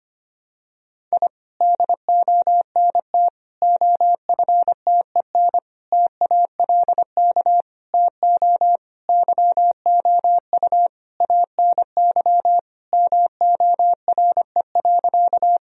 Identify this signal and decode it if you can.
Morse code, heard as a keyed tone, about 25 wpm, 700 Hz. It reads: I DONT OFTEN TALK TO YOU ANY MORE.
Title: Sentences from Top 1000 Words - Review Part 1 - 25wpm